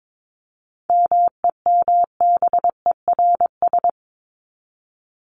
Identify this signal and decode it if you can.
Morse code, heard as a keyed tone, about 22 wpm, 700 Hz. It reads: MEMBERS
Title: ES:S Suffix - Top 100 Words with Base Words 22wpm